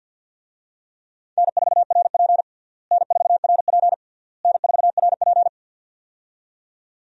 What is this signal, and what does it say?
Morse code, heard as a keyed tone, about 50 wpm, 700 Hz. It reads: N4RP N4RP N4RP
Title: USA Call Signs - 4 Characters - Repeated 3x - 50wpm